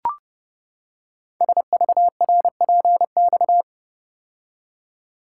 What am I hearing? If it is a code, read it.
Morse code, heard as a keyed tone, about 30 wpm, 700 Hz. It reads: SVRPX
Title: Code Groups - 5 Letters - 30wpm